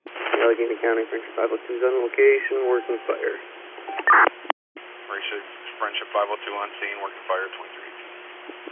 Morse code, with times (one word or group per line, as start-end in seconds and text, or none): none